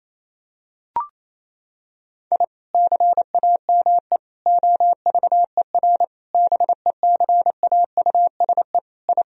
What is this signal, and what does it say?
Morse code, heard as a keyed tone, about 27 wpm, 700 Hz. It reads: I CAME OVER BECAUSE I
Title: Sentences from Top 1000 Words - Review Part 3 - 28wpm